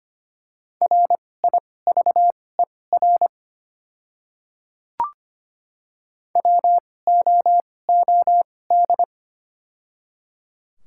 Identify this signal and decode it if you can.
Morse code, heard as a keyed tone, about 25 wpm, spaced as for 18 wpm, 700 Hz. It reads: RIVER WOOD
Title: 2x Character Spacing - Top 500 Words - 25wpm